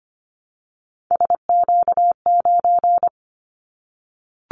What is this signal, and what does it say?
Morse code, heard as a keyed tone, about 25 wpm, 700 Hz. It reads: SQ9
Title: Callsign Prefixes for Top 25 DXCC - Repeated 3x - 25wpm